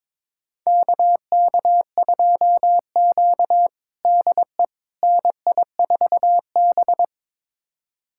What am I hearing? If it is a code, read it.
Morse code, heard as a keyed tone, about 22 wpm, 700 Hz. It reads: KK2Q DE NI4B
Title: USA Callsign DE Callsign - 4 Characters - 22wpm